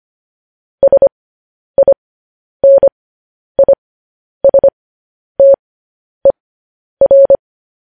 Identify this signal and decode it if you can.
Morse code, heard as a keyed tone, about 25 wpm, spaced as for 10 wpm, 550 Hz. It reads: SINISTER